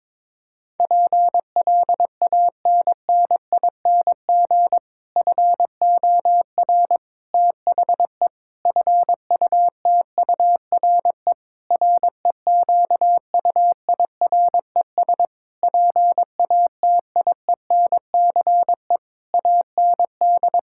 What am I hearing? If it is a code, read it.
Morse code, heard as a keyed tone, about 22 wpm, 700 Hz. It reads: PLANNING FOR THE FUTURE REQUIRES PATIENCE AND
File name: Difficult Sentences with Hints - 22wpm